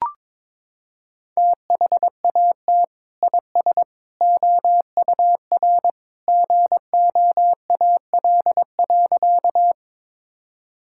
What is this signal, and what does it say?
Morse code, heard as a keyed tone, about 22 wpm, 700 Hz. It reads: THAT IS OUR GOAL.
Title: Sentences from Top 1000 Words - Review Part 5 - 22wpm